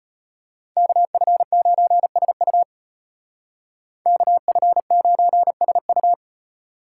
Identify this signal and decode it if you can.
Morse code, heard as a keyed tone, about 33 wpm, 700 Hz. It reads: KF9SU KF9SU